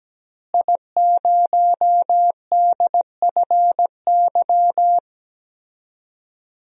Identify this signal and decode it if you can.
Morse code, heard as a keyed tone, about 17 wpm, 700 Hz. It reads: I0DFY
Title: DXCC-100 Call Signs - 5 Characters - Repeated 3x - 17wpm